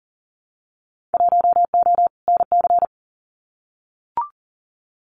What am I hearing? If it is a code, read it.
Morse code, heard as a keyed tone, about 40 wpm, 700 Hz. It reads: 1O NC
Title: Winter Field Day Exchanges 40wpm